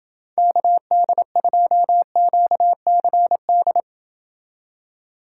Morse code, heard as a keyed tone, about 28 wpm, 700 Hz. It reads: KD2QCD